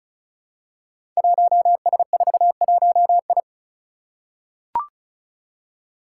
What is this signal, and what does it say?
Morse code, heard as a keyed tone, about 36 wpm, 700 Hz. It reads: 1S41I
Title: Five Challenging Letter-Number Combinations 35wpm